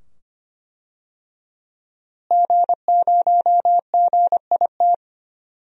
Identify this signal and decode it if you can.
Morse code, heard as a keyed tone, about 25 wpm, 700 Hz. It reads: G0GIT
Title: DXCC-100 Call Signs - 5 Characters - Repeated 2x - 25wpm